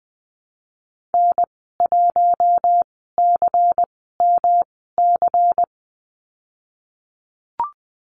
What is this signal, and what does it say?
Morse code, heard as a keyed tone, about 20 wpm, spaced as for 15 wpm, 700 Hz. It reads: N1CMC
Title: 140 Lesson 23g - Letter f - Call Signs 20wpm